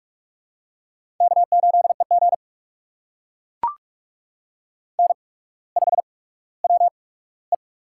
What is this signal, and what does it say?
Morse code, heard as a keyed tone, about 43 wpm, spaced as for 19 wpm, 700 Hz. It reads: K8EG N5WE
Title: USA Call Signs - 4 Characters - 8x Character Spacing - 45wpm